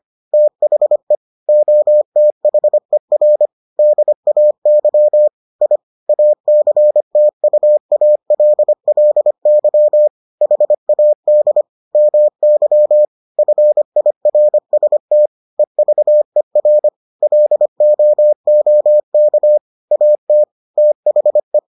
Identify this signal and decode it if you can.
Morse code, heard as a keyed tone, about 25 wpm, 600 Hz. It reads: THE OTHER DAY I ACTUALLY HAD MY FIRST EVER LOOK AT THE